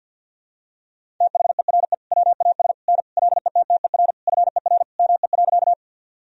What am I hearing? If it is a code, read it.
Morse code, heard as a keyed tone, about 52 wpm, 700 Hz. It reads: THERE WAS N LETTER FER ME.
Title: Sentences from Top 300 Words and Abbreviations 50wpm